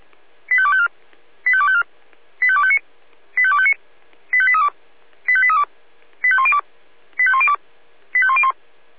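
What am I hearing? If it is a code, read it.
Five-tone selective call (ZVEI), heard as quick runs of five tones, tones of about 70 ms. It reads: 85254 85254 85259 85259 85832 85832 85102 85102 85101
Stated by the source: ZVEI five-tone selective call received by SDR.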